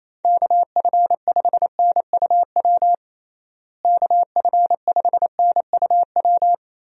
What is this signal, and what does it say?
Morse code, heard as a keyed tone, about 28 wpm, 700 Hz. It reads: KF5NUW KF5NUW
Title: USA Call Signs - 6 Characters - Repeated 2x - 28wpm